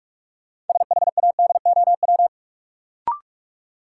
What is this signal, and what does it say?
Morse code, heard as a keyed tone, about 46 wpm, 700 Hz. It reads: SHADOW
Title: Sets of 5 Words Encore - New Word Drill - 45wpm